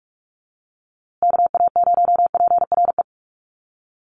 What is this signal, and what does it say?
Morse code, heard as a keyed tone, about 44 wpm, 700 Hz. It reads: KA0PRE
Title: USA Call Signs Repeated 3x 45wpm